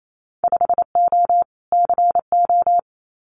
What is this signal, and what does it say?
Morse code, heard as a keyed tone, about 28 wpm, 700 Hz. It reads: 5O CO
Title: Winter Field Day Exchanges Encore 28wpm